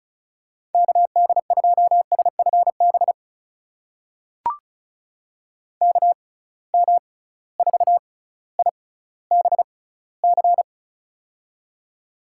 Morse code, heard as a keyed tone, about 36 wpm, spaced as for 18 wpm, 700 Hz. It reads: KD2SFB KM4IBC
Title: USA Call Signs - 6 Characters - 6x Character Spacing - 35wpm